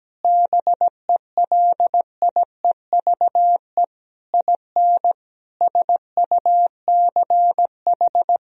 Morse code, heard as a keyed tone, about 17 wpm, 700 Hz. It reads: BELIEVE IN SUCH
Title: Sentences from Top 1000 Words - Review Part 2 - 17wpm